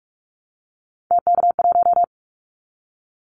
Morse code, heard as a keyed tone, about 44 wpm, 700 Hz. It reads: TK1
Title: Callsign Prefixes for Top 100 DXCC - 45wpm